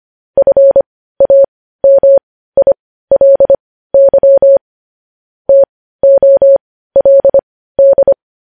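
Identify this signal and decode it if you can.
Morse code, heard as a keyed tone, about 25 wpm, spaced as for 15 wpm, 550 Hz. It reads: FAMILY TOLD
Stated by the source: Morse code news headlines sent at 15.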